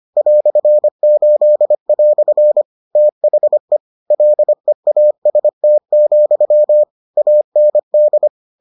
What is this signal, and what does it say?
Morse code, heard as a keyed tone, about 25 wpm, 600 Hz. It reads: "8" THE LEAST, AND